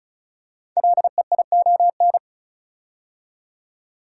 Morse code, heard as a keyed tone, about 36 wpm, 690 Hz. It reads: LEION
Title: Code Groups - 5 Letter:Numbers - 35wpm